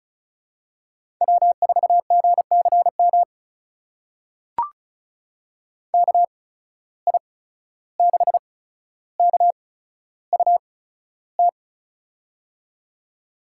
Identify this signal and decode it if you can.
Morse code, heard as a keyed tone, about 34 wpm, spaced as for 14 wpm, 700 Hz. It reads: W4GCM KI6KUT
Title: USA Call Signs - 8x Character Spacing - 35wpm